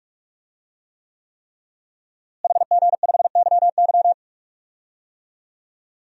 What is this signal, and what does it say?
Morse code, heard as a keyed tone, about 46 wpm, 700 Hz. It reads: HG5YY
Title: DXCC-100 Call Signs - 5 Characters - Repeated 3x - 45wpm